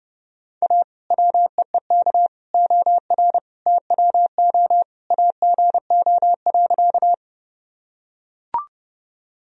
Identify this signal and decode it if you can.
Morse code, heard as a keyed tone, about 31 wpm, 700 Hz. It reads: A WEEK OR TWO AGO.
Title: Sentences from Top 1000 Words - Review Part 3 - 30wpm